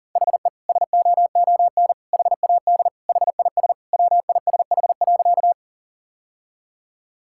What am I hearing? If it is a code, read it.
Morse code, heard as a keyed tone, about 40 wpm, 700 Hz. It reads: HE SOON HAD HIS WISH.
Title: Sentences from Top 1600 Words 40wpm